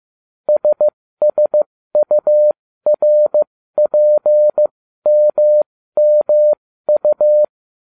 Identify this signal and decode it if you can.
Morse code, heard as a keyed tone, about 15 wpm, spaced as for 13 wpm, 600 Hz. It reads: SSURPMMU